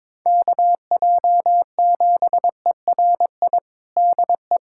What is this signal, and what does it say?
Morse code, heard as a keyed tone, about 22 wpm, 700 Hz. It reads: KJ7ERI DE
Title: USA Callsign DE Callsign 22wpm